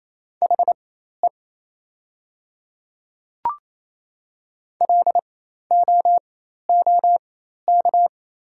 Morse code, heard as a keyed tone, about 28 wpm, spaced as for 13 wpm, 700 Hz. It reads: HE LOOK
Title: 4x Character Spacing - Top 300 Words - 28wpm